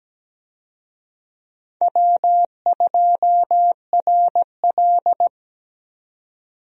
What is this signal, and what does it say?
Morse code, heard as a keyed tone, about 17 wpm, 700 Hz. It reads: W2RL
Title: USA Call Signs - 4 Characters - 17wpm